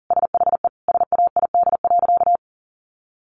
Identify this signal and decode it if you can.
Morse code, heard as a keyed tone, about 40 wpm, 700 Hz. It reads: SHE SAID.
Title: Sentences from Top 700 Words 40wpm